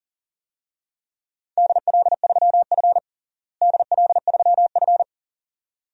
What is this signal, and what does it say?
Morse code, heard as a keyed tone, about 40 wpm, 700 Hz. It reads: DL3F DL3F
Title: DXCC-100 Call Signs - 4 Characters - Repeated 2x - 40wpm